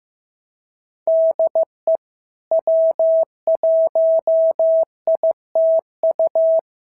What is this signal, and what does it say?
Morse code, heard as a keyed tone, about 16 wpm, 650 Hz. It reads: DE W1ITU
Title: SOTA QSO Practice 15wpm